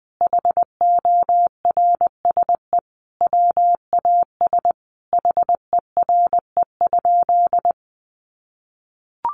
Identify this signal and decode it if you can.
Morse code, heard as a keyed tone, about 20 wpm, 700 Hz. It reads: HORSE WAS HERE?
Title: Sentences from Top 1500 Words 20wpm